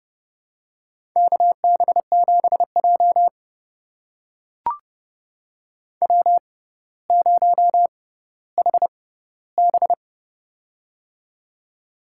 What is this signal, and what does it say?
Morse code, heard as a keyed tone, about 30 wpm, spaced as for 15 wpm, 700 Hz. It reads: KB7J W0HB
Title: USA Call Signs - 4 Characters - 6x Character Spacing - 30wpm